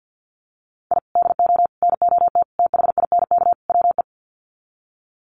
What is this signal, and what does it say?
Morse code, heard as a keyed tone, about 50 wpm, 700 Hz. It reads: I DO NOT THINK WE
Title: Sets of 5 Words 50wpm